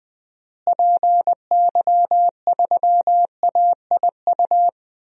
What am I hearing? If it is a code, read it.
Morse code, heard as a keyed tone, about 20 wpm, 700 Hz. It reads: PY3AIU
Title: DXCC-100 Call Signs - 6 Characters - Repeated 3x - 20wpm